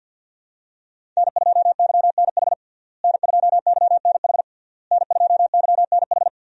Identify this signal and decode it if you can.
Morse code, heard as a keyed tone, about 51 wpm, 690 Hz. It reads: N2YNH N2YNH N2YNH
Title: USA Call Signs - 5 Characters - Repeated 3x - 50wpm